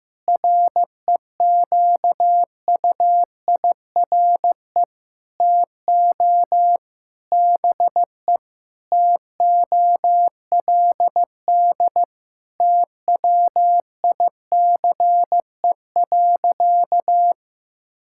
Morse code, heard as a keyed tone, about 16 wpm, 700 Hz. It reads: REQUIRE TO BE TOLD TWICE.
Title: Sentences from Top 1600 Words 15wpm